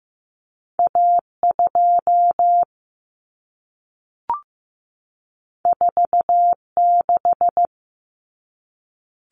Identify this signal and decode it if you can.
Morse code, heard as a keyed tone, about 15 wpm, 700 Hz. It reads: A2 46
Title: Common Cut Numbers - 2-Digit 15wpm